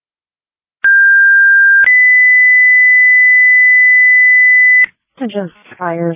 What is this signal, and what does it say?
Two-tone page, heard as a one-second tone, then a three-second tone, about 1600 then 2000 Hz, with speech. It is heard from 0.8 to 4.9 s.